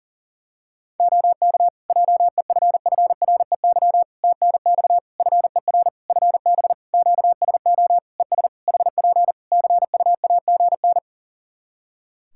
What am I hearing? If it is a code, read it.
Morse code, heard as a keyed tone, about 40 wpm, 700 Hz. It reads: OK JEFFREY TNX FER FB QSO ES HP CUAGN
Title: Practice QSO - Exchange 4 - Ending - 40wpm